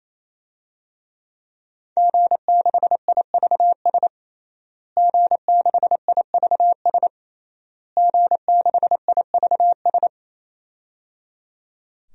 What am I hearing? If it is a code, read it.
Morse code, heard as a keyed tone, about 27 wpm, 700 Hz. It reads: G6IVS G6IVS G6IVS
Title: DXCC-100 Call Signs - 5 Characters - Repeated 3x - 28wpm